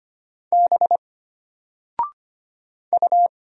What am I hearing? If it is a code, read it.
Morse code, heard as a keyed tone, about 25 wpm, 700 Hz. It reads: B U
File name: Single-Letter-Number-25wpm